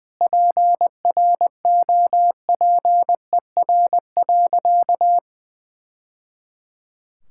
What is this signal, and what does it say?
Morse code, heard as a keyed tone, about 20 wpm, 700 Hz. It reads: PROPER.
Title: Sentences from Top 1000 Words - Review Part 5 - 20wpm